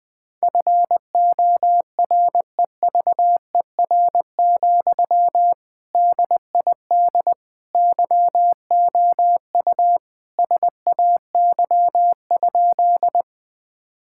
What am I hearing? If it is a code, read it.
Morse code, heard as a keyed tone, about 20 wpm, 700 Hz. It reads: FOREVER, DID YOU SAY?
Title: Sentences from Top 1600 Words 20wpm